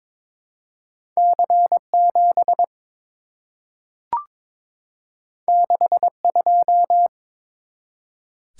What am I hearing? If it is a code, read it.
Morse code, heard as a keyed tone, about 22 wpm, 700 Hz. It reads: C7 62
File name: practice / 22wpm / Two Letter-Number Combinations - 22wpm